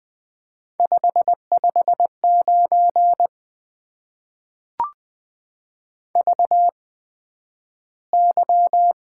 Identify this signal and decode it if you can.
Morse code, heard as a keyed tone, about 20 wpm, spaced as for 11 wpm, 700 Hz. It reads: 559 VY
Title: Top 90 QSO Elements - 8x Character Spacing - 20wpm